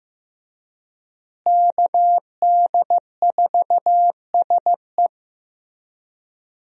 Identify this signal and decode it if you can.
Morse code, heard as a keyed tone, about 15 wpm, 700 Hz. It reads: KD4SE
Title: USA Call Signs - 5 Characters - Repeated 2x - 15wpm